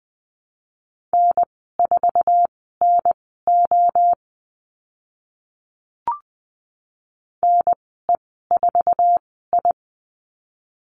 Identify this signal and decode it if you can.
Morse code, heard as a keyed tone, about 20 wpm, spaced as for 15 wpm, 700 Hz. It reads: N4NO NE4I
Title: 062 Lesson 11g - Letter h - Call Signs 20wpm